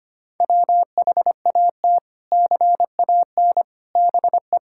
Morse code, heard as a keyed tone, about 25 wpm, 700 Hz. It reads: WHAT CAN BE
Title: Sentences from Top 1000 Words - Review Part 6 - 25wpm